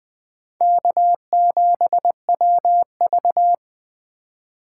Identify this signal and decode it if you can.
Morse code, heard as a keyed tone, about 21 wpm, 700 Hz. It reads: K7WV
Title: Speed Racing - USA Call Signs 17wpm